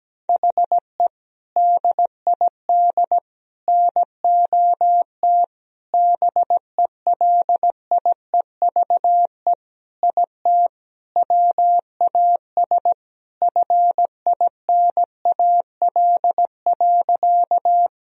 Morse code, heard as a keyed tone, about 17 wpm, 700 Hz. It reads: HE DID NOT BELIEVE IT WAS FINAL.
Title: Sentences from Top 1000 Words - Review Part 2 - 17wpm